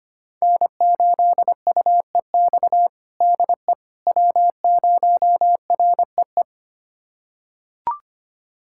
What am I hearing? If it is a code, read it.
Morse code, heard as a keyed tone, about 25 wpm, 700 Hz. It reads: N8UEX DE W0REE
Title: USA Callsign DE Callsign - 5 Characters - 25wpm